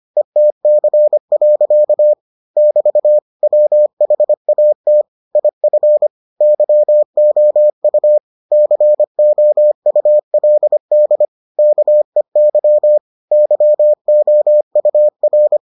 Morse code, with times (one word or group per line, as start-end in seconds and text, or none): 0.2-2.1 ETC.
2.6-3.2 <BT>
3.4-5.0 WHAT
5.3-6.1 IF
6.4-8.2 YOU
8.5-11.2 COULD
11.6-13.0 KEY
13.3-15.6 YOUR